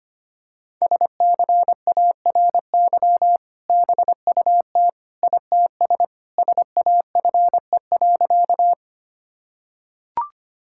Morse code, heard as a keyed tone, about 25 wpm, 700 Hz. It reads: SCARY BUT ITS SAFE.
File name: practice / 25wpm / Sentences with 2-Word Phrases - 25wpm